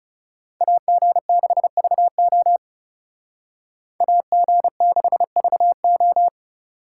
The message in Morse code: AG6VO AG6VO

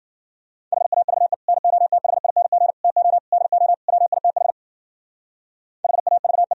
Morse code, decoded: HAVE NOTHING TO DO WITH HAVE

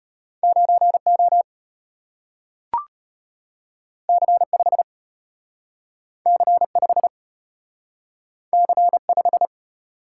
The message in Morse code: 9O C5 C5 C5

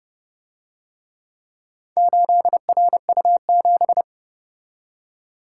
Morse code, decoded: 8RU7

